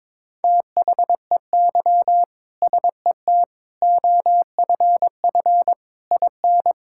Morse code, heard as a keyed tone, about 22 wpm, 700 Hz. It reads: THEY SET OFF IN
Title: Sentences from Top 1000 Words - Review Part 5 - 22wpm